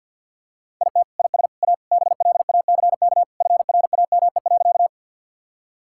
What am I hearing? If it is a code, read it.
Morse code, heard as a keyed tone, about 50 wpm, 700 Hz. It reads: IT IS A BLACK FRAME.
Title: Sentences from Top 1000 Words - Review Part 6 - 50wpm